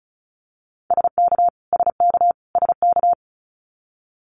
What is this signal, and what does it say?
Morse code, heard as a keyed tone, about 36 wpm, 700 Hz. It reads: SK SK SK
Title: Canadian Provinces and Territories - Repeated 3x - 35wpm